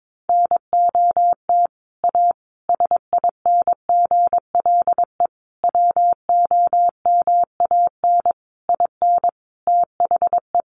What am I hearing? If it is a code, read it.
Morse code, heard as a keyed tone, about 22 wpm, 700 Hz. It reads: NOT A SINGLE WOMAN IN THE
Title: Sentences from Top 1100 Words 22wpm